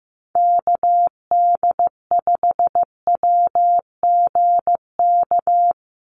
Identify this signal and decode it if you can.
Morse code, heard as a keyed tone, about 15 wpm, 700 Hz. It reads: KD5WGK
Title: USA Call Signs 15wpm